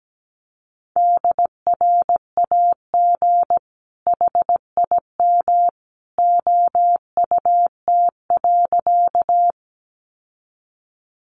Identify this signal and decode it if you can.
Morse code, heard as a keyed tone, about 17 wpm, 700 Hz. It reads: DRAG HIM OUT.
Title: Sentences from Top 1500 Words 17wpm